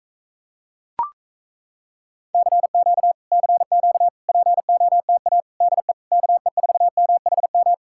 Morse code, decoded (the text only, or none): CQ CQ POTA DE KE4MHM